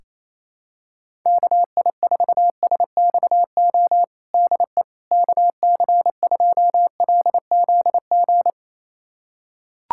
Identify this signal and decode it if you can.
Morse code, heard as a keyed tone, about 27 wpm, 700 Hz. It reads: KI4SXO DE KC2LZG
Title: USA Callsign DE Callsign 28wpm